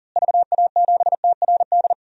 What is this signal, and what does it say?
Morse code, heard as a keyed tone, about 40 wpm, 700 Hz. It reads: VA7TRD
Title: DXCC-100 Call Signs - 6 Characters - Repeated 2x - 40wpm